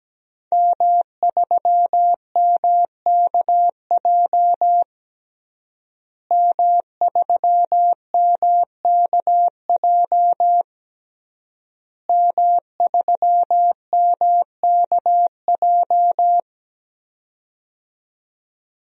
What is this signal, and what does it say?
Morse code, heard as a keyed tone, about 17 wpm, 700 Hz. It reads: M3MKJ M3MKJ M3MKJ
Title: DXCC-100 Call Signs - 5 Characters - Repeated 3x - 17wpm